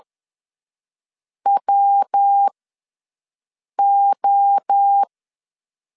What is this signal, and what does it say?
Morse code, heard as a keyed tone, about 10 wpm, spaced as for 5 wpm, 800 Hz. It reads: WO
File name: LEZIONI / 9 2867 ws 10wpm 4215